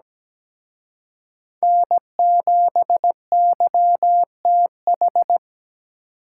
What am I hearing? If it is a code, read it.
Morse code, heard as a keyed tone, about 20 wpm, 700 Hz. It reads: N7YTH
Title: Speed Racing - USA 1x3 Call Signs 17wpm